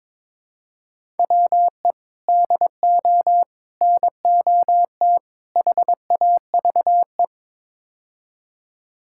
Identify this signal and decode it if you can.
Morse code, heard as a keyed tone, about 22 wpm, 700 Hz. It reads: WE DO NOT HAVE
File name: practice / 22wpm / Build A Phrase - 5-Word Phrases - 22wpm